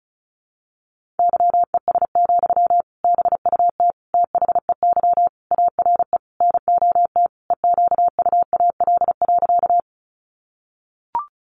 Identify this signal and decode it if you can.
Morse code, heard as a keyed tone, about 34 wpm, 700 Hz. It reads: YES, BUT THEY ARE NOT EQUAL.
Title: Sentences from Top 1400 Words 35wpm